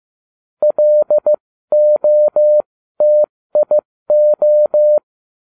Morse code, heard as a keyed tone, about 15 wpm, spaced as for 13 wpm, 600 Hz. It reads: LOTIO